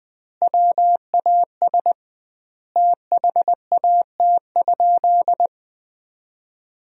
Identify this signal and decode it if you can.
Morse code, heard as a keyed tone, about 20 wpm, 700 Hz. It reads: WAS THAT?
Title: 2x Word-Spacing - Sentences from Top 100 Words - 20wpm